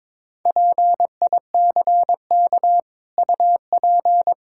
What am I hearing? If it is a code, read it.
Morse code, heard as a keyed tone, about 22 wpm, 700 Hz. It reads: PICK UP
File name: practice / 22wpm / Easy Sentences with Hints - 22wpm